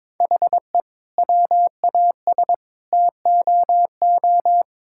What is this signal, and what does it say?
Morse code, heard as a keyed tone, about 21 wpm, 700 Hz. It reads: HE WAS TOO
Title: Sentences from Top 1000 Words - Review Part 6 - 22wpm